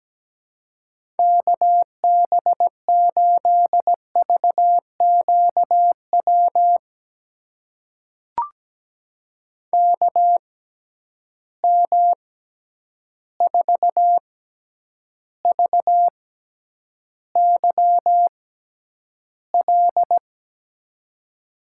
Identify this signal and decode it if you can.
Morse code, heard as a keyed tone, about 17 wpm, spaced as for 9 wpm, 700 Hz. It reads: KB8VQW KM4VYL
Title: USA Call Signs - 6 Characters - 6x Character Spacing - 17wpm